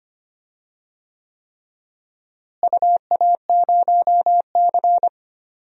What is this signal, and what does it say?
Morse code, heard as a keyed tone, about 25 wpm, 700 Hz. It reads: UA0C
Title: DXCC-100 Call Signs - 4 Characters - Repeated 3x - 25wpm